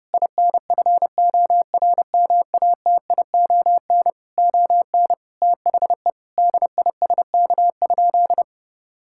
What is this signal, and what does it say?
Morse code, heard as a keyed tone, about 32 wpm, 700 Hz. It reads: INFORMATION ON THE DISK?
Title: Sentences from Top 1000 Words - Review Part 6 - 30wpm